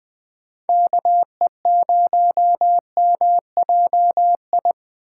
Morse code, heard as a keyed tone, about 19 wpm, 700 Hz. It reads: KE0MJI
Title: USA Call Signs - 6 Characters - 8x Character Spacing - 20wpm